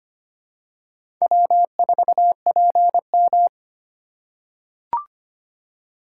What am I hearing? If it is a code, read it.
Morse code, heard as a keyed tone, about 25 wpm, 700 Hz. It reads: W4PM
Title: USA Call Signs - 4 Characters - 4x Character Spacing - 25wpm